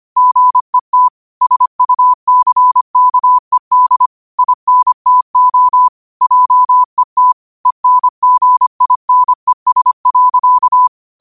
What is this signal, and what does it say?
Morse code, heard as a keyed tone, about 25 wpm, 1000 Hz. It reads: GET SUCKED INTO JET ENGINES.
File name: qotdfiles / 25wpm